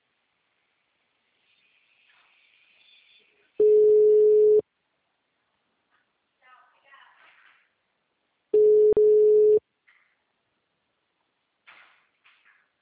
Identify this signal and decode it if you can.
Ringback tone (Europe, 425 Hz), with speech.
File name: out- -2004- - - .42999